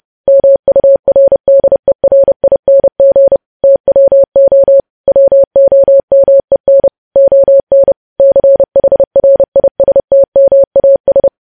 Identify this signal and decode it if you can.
Morse code, heard as a keyed tone, about 30 wpm, 550 Hz. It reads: MURDERING TWO WOMEN ON CHRISTMAS
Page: news headlines in morse code at 30 wpm